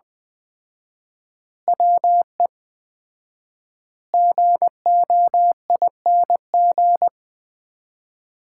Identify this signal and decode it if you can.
Morse code, heard as a keyed tone, about 20 wpm, 700 Hz. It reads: WE GOING
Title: 4x Word-Spacing - Sets of 5 Words - 20wpm